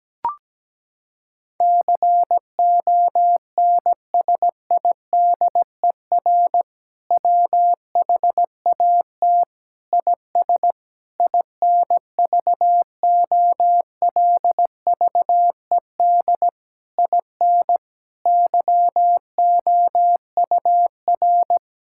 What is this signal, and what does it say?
Morse code, heard as a keyed tone, about 17 wpm, 700 Hz. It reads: CONSIDER WHAT IS INVOLVED IN YOUR
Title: Sentences from Top 1000 Words - Review Part 5 - 17wpm